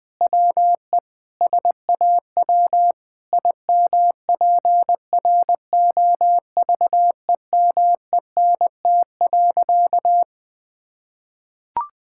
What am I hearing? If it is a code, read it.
Morse code, heard as a keyed tone, about 20 wpm, 700 Hz. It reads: WE SAW IMPROVEMENT.
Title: Sentences with 8-Word Phrases - 20wpm